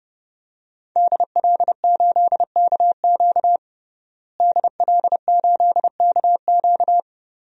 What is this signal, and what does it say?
Morse code, heard as a keyed tone, about 30 wpm, 700 Hz. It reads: DL8KQ DL8KQ
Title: DXCC-100 Call Signs - 5 Characters - Repeated 3x - 30wpm